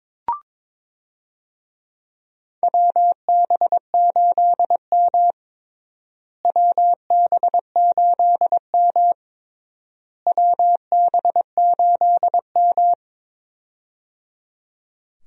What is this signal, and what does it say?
Morse code, heard as a keyed tone, about 22 wpm, 700 Hz. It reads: WB8M WB8M WB8M
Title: USA Call Signs - 4 Characters - Repeated 3x - 22wpm